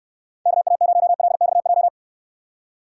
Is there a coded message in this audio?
Yes: Morse code from 0.5 to 1.9 s, heard as a keyed tone, about 69 wpm, 700 Hz. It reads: KA0LXJ